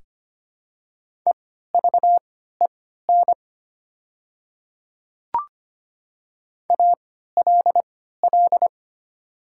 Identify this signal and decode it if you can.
Morse code, heard as a keyed tone, about 25 wpm, spaced as for 14 wpm, 700 Hz. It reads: EVEN ALL
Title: 3x Character Spacing - Top 200 Words - 25wpm